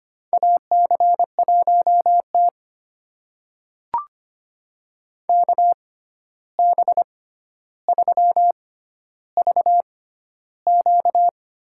Morse code, heard as a keyed tone, about 25 wpm, spaced as for 12 wpm, 700 Hz. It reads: AC1T KB3VQ
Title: USA Call Signs - 6x Character Spacing - 25wpm